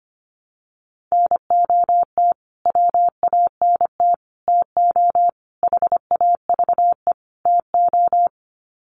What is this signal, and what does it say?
Morse code, heard as a keyed tone, about 25 wpm, 700 Hz. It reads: NOT WANT TO HAVE TO
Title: Sets of 5 Words Encore2 25wpm